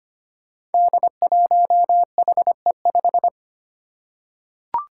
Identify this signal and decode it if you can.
Morse code, heard as a keyed tone, about 25 wpm, 700 Hz. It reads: D1HE5
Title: Five Challenging Letter-Number Combinations 25wpm